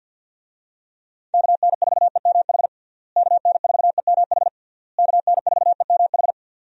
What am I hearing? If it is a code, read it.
Morse code, heard as a keyed tone, about 50 wpm, 700 Hz. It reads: KN4EMH KN4EMH KN4EMH
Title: USA Call Signs - 6 Characters - Repeated 3x - 50wpm